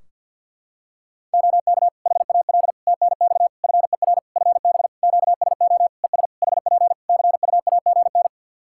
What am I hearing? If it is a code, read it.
Morse code, heard as a keyed tone, about 53 wpm, 700 Hz. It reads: OK HAL TNX FER FB QSO ES HP CUAGN
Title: Practice QSO - Exchange 4 - Ending - 50wpm